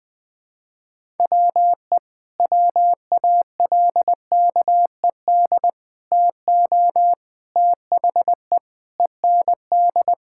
Morse code, heard as a keyed tone, about 21 wpm, 700 Hz. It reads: WE WALKED TO THE END
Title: Sentences with 5-Word Phrases - 20wpm